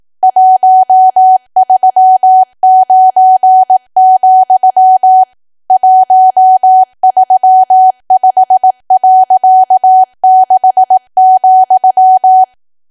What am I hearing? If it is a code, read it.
Morse code, heard as a keyed tone, about 18 wpm, 750 Hz. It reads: 139, 135.6,
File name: cwa / W1AW-Level2 / 18WPM